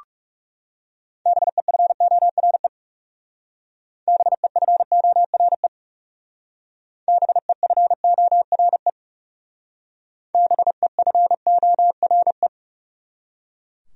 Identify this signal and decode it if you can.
Morse code, heard as a keyed tone, about 38 wpm, 700 Hz. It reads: BEFORE BEFORE BEFORE BEFORE